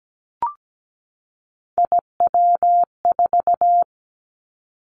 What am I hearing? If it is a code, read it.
Morse code, heard as a keyed tone, about 17 wpm, 700 Hz. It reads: IW4
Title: Callsign Prefixes for Top 25 DXCC - 17wpm